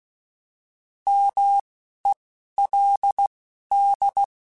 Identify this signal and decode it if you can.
Morse code, heard as a keyed tone, about 16 wpm, 800 Hz. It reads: M E L D